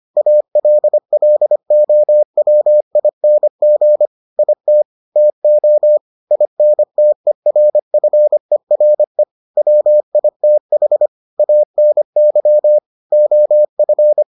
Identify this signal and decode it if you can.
Morse code, heard as a keyed tone, about 25 wpm, 600 Hz. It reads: ALLOWING IT TO INTERFERE WITH ANY OF